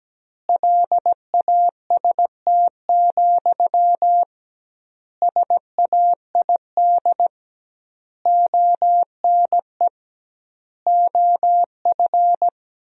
Morse code, heard as a keyed tone, about 17 wpm, 700 Hz. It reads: LAST, SAID ONE OF
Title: 2x Word-Spacing - Sentences from Top 300 Words - 17wpm